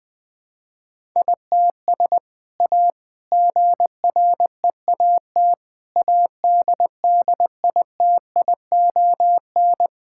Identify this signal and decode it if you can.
Morse code, heard as a keyed tone, about 21 wpm, 700 Hz. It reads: ITS A GREAT ADDITION
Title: Sentences from Top 1000 Words - Review Part 6 - 20wpm